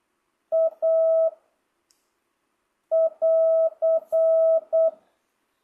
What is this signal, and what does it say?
Morse code, heard as a keyed tone, about 8 wpm, 660 Hz. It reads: A <AR>
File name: jh8cbh / 2025cw